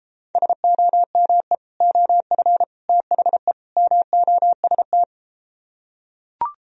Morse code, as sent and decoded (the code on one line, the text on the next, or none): ... --- -- . / --- ..-. / - .... / -- --- ... -
SOME OF THE MOST